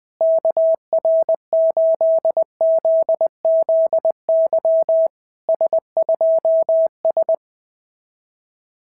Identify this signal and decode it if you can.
Morse code, heard as a keyed tone, about 21 wpm, 690 Hz. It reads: KR8ZZY S2S
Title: SOTA QSO Practice 20wpm